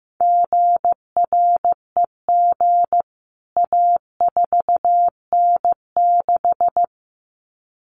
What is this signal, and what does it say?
Morse code, heard as a keyed tone, about 15 wpm, 700 Hz. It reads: GREG A4N6
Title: CWT Exchange 15wpm